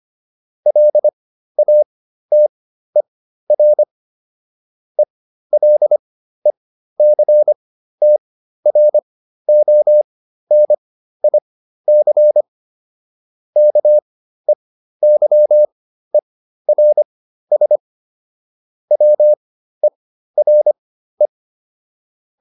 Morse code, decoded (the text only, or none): LATER ELECTRONIC KEYERS WERE